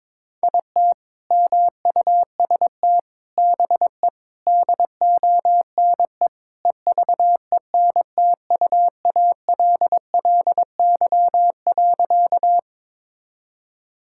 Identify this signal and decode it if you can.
Morse code, heard as a keyed tone, about 22 wpm, 700 Hz. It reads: IT MUST BE DONE EVENTUALLY.
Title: Sentences from Top 1000 Words - Review Part 6 - 22wpm